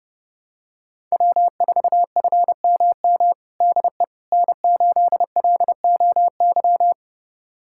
Morse code, heard as a keyed tone, about 32 wpm, 700 Hz. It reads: W4FMM DE N8LOY
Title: USA Callsign DE Callsign - 5 Characters - 30wpm